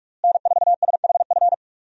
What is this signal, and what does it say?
Morse code, heard as a keyed tone, about 46 wpm, 700 Hz. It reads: N4SHF
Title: USA Call Signs - 5 Characters - Repeated 2x - 45wpm